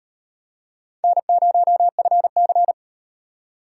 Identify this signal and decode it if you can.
Morse code, heard as a keyed tone, about 38 wpm, 700 Hz. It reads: N0FC